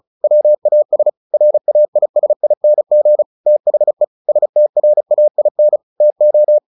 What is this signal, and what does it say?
Morse code, heard as a keyed tone, about 34 wpm, 600 Hz. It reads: WAS RAISING THE STRAIN TO
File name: the old man and the sea 2 35wpm